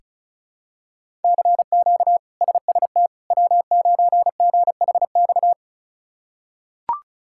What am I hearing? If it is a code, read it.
Morse code, heard as a keyed tone, about 34 wpm, 700 Hz. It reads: CQ SST W9GHX
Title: K1USN SST QSO Practice - 35wpm